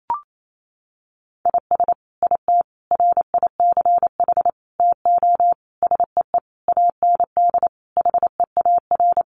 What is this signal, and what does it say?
Morse code, heard as a keyed tone, about 28 wpm, 700 Hz. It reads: IS IT RICH TO SEE AND HEAR
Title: Sentences from Top 1100 Words 28wpm